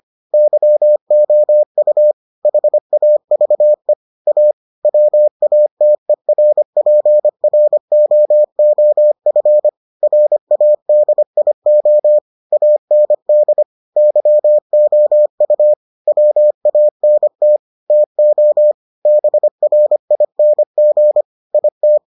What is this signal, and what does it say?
Morse code, heard as a keyed tone, about 25 wpm, 600 Hz. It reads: YOU HAVE A WATERPROOF RADIO AND YOU WANT TO BRING IT